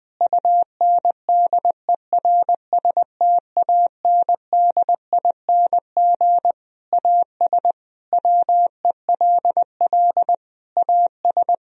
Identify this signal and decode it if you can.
Morse code, heard as a keyed tone, about 21 wpm, 700 Hz. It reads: UNDERSTANDING AS WELL AS
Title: Sentences from Top 1000 Words - Review Part 3 - 20wpm